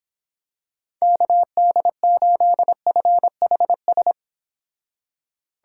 Morse code, heard as a keyed tone, about 26 wpm, 700 Hz. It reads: KD8FHS